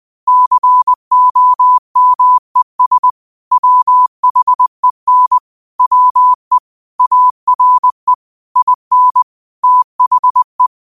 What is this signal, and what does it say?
Morse code, heard as a keyed tone, about 20 wpm, 1000 Hz. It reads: COMES WHEN WE ARE IN THE
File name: 20wpm